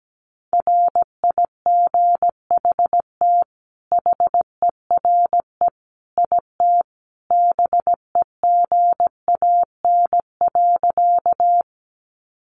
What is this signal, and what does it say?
Morse code, heard as a keyed tone, about 17 wpm, 700 Hz. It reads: RIGHT HERE IT BEGAN.
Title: Sentences from Top 300 Words 17wpm